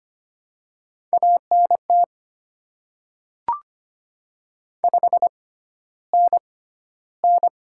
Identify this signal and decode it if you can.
Morse code, heard as a keyed tone, about 25 wpm, spaced as for 13 wpm, 700 Hz. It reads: ANT 5NN
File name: Top 20 QSO Elements - 6x Character Spacing - 25wpm